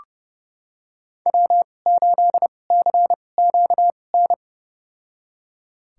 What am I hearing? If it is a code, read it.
Morse code, heard as a keyed tone, about 31 wpm, spaced as for 22 wpm, 700 Hz. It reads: W8CQN
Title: USA Call Signs - 5 Characters - 2x Character Spacing - 30wpm